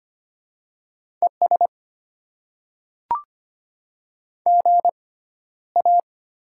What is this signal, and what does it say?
Morse code, heard as a keyed tone, about 25 wpm, spaced as for 13 wpm, 700 Hz. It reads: ES GA
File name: Top 20 QSO Elements - 6x Character Spacing - 25wpm